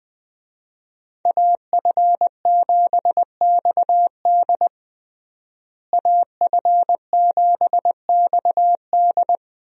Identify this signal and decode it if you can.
Morse code, heard as a keyed tone, about 20 wpm, 700 Hz. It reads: AF7XD AF7XD
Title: USA Call Signs - 5 Characters - Repeated 2x - 20wpm